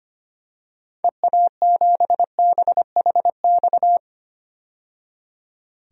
Morse code, heard as a keyed tone, about 25 wpm, 700 Hz. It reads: EA7BHX
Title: DXCC-100 Call Signs - 6 Characters - 25wpm